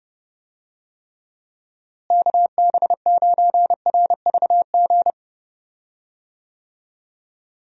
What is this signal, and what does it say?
Morse code, heard as a keyed tone, about 30 wpm, 700 Hz. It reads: KB9RVG
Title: USA Call Signs - 6 Characters - Repeated 2x - 30wpm